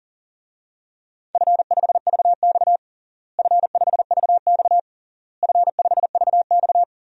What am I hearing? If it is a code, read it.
Morse code, heard as a keyed tone, about 40 wpm, 700 Hz. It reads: F5VX F5VX F5VX
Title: DXCC-100 Call Signs - 4 Characters - Repeated 3x - 40wpm